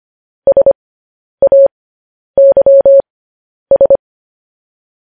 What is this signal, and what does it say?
Morse code, heard as a keyed tone, about 25 wpm, spaced as for 10 wpm, 550 Hz. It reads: SAYS
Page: News Headlines in Morse Code at 10 WPM